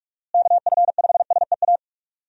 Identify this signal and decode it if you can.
Morse code, heard as a keyed tone, about 43 wpm, 700 Hz. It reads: KF5SEA
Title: USA Call Signs - 6 Characters - Repeated 3x - 45wpm